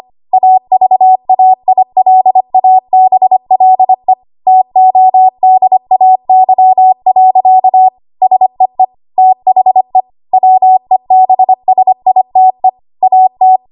Morse code, heard as a keyed tone, about 25 wpm, 750 Hz. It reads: AVAILABLE TODAY. SEE THE WEBSITE AT
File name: Archive / 25 WPM / 25WPM